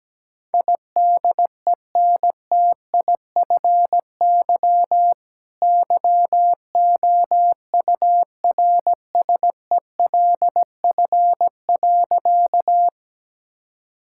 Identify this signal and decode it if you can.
Morse code, heard as a keyed tone, about 17 wpm, 700 Hz. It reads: IDENTIFY YOURSELF.
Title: Sentences from Top 1000 Words - Review Part 6 - 17wpm